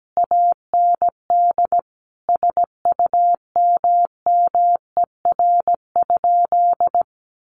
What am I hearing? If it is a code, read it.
Morse code, heard as a keyed tone, about 17 wpm, 700 Hz. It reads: AND SUMMER?
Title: Sentences from Top 1200 Words 17wpm